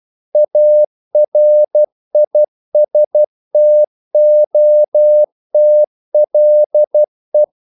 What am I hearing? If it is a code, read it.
Morse code, heard as a keyed tone, about 12 wpm, 600 Hz. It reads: ARISTOTLE
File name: CW / Practice / EnglishQuotations0312